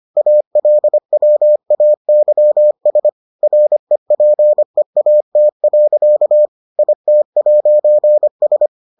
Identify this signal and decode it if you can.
Morse code, heard as a keyed tone, about 25 wpm, 600 Hz. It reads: ALWAYS REPEAT. IT'S